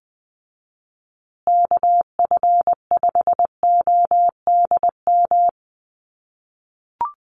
Morse code, heard as a keyed tone, about 20 wpm, 700 Hz. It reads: KF5ODM
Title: USA Call Signs Repeated 3x 20wpm